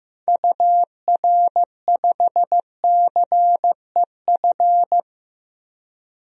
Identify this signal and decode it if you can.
Morse code, heard as a keyed tone, about 15 wpm, 700 Hz. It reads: UR5CEF
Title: DXCC-100 Call Signs - 6 Characters - Repeated 3x - 15wpm